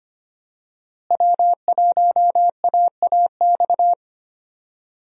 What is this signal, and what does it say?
Morse code, heard as a keyed tone, about 25 wpm, 700 Hz. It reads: W1AAX
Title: USA Call Signs - 5 Characters - Repeated 2x - 25wpm